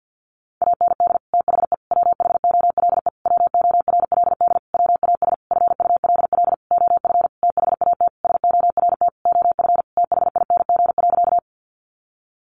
Morse code, heard as a keyed tone, about 50 wpm, 700 Hz. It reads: AND THE WHOLE WORLD WAS FULL OF THAT SORT OF THING.
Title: Sentences from Top 1300 Words 50wpm